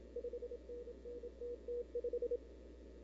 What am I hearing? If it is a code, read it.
Morse code, heard as a keyed tone, about 27 wpm, 470 Hz. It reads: 5NN TT5
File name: CW / 2005-k5tr arrl cw dx -00:00